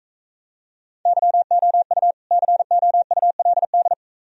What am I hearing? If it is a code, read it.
Morse code, heard as a keyed tone, about 42 wpm, 700 Hz. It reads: YOU COULD